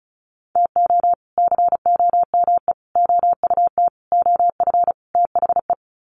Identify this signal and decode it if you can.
Morse code, heard as a keyed tone, about 34 wpm, 700 Hz. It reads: TO COME OUT OF THE
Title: Sets of 5 Words Encore2 35wpm